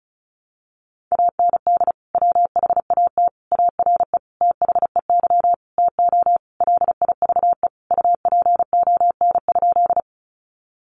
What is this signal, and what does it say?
Morse code, heard as a keyed tone, about 34 wpm, 700 Hz. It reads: AND WHAT ARE THEY TO LIVE UPON?
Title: Sentences from Top 1100 Words 35wpm